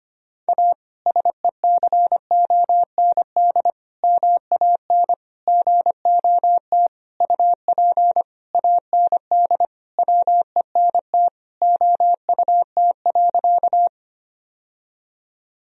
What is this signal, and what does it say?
Morse code, heard as a keyed tone, about 25 wpm, 700 Hz. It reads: A SECOND MAN GOT UP AND WENT OUT.
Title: Sentences from Top 1000 Words - Review Part 1 - 25wpm